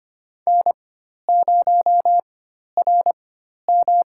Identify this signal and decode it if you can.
Morse code, heard as a keyed tone, about 25 wpm, spaced as for 12 wpm, 700 Hz. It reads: N0RM
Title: USA Call Signs - 4 Characters - 4x Character Spacing - 25wpm